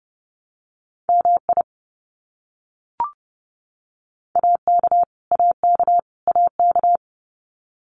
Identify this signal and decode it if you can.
Morse code, heard as a keyed tone, about 30 wpm, 700 Hz. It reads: MI AK AK AK
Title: US States Repeated 3x 30wpm